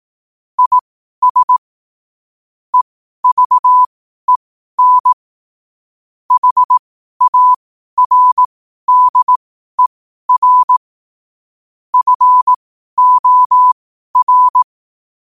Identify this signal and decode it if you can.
Morse code, heard as a keyed tone, about 18 wpm, spaced as for 12 wpm, 1000 Hz. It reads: IS EVEN HARDER FOR
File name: qotdfiles / 12wpm